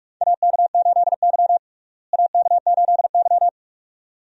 Morse code, heard as a keyed tone, about 42 wpm, 700 Hz. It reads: AK8Y AK8Y
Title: USA Call Signs - 4 Characters - Repeated 2x - 45wpm